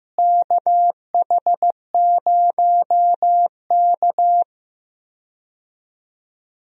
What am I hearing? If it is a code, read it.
Morse code, heard as a keyed tone, about 15 wpm, 700 Hz. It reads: KH0K
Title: USA Call Signs - 4 Characters - Repeated 2x - 15wpm